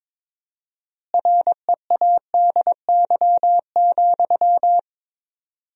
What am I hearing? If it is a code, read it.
Morse code, heard as a keyed tone, about 22 wpm, 700 Hz. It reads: READY,
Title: 4x Word-Spacing - Sentences from Top 500 Words - 22wpm